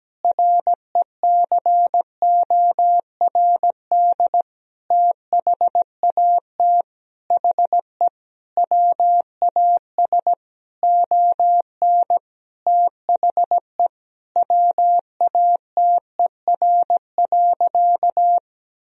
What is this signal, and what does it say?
Morse code, heard as a keyed tone, about 17 wpm, 700 Hz. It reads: RECORD THAT HE WAS ON THE WATER.
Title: Sentences from Top 500 Words and Abbreviations 17wpm